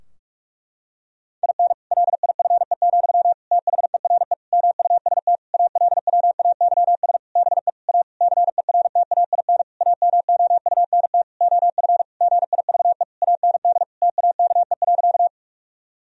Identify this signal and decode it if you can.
Morse code, heard as a keyed tone, about 41 wpm, 700 Hz. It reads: IN LIFE, THERE MUST ALWAYS BE A CERTAIN AMOUNT OF GIVE AND TAKE.
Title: Sentences from Top 1000 Words - Review Part 4 - 45wpm